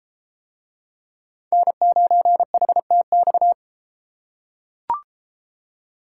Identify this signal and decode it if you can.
Morse code, heard as a keyed tone, about 30 wpm, 700 Hz. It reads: N9HTX